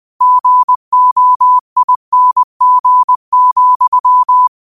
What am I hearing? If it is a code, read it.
Morse code, heard as a keyed tone, about 20 wpm, 1000 Hz. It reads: GOING,
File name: qotdfiles / 20wpm